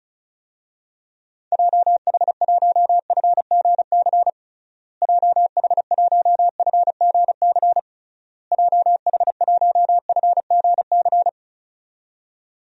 Morse code, heard as a keyed tone, about 34 wpm, 700 Hz. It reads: JH1FGC JH1FGC JH1FGC
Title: DXCC-100 Call Signs - 6 Characters - Repeated 3x - 35wpm